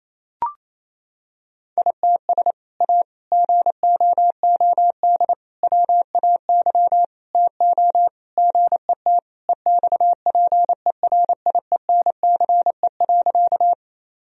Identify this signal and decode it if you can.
Morse code, heard as a keyed tone, about 28 wpm, 690 Hz. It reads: ITS A GOOD WAY TO GET EXPERIENCE.
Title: Sentences from Top 1000 Words - Review Part 3 - 28wpm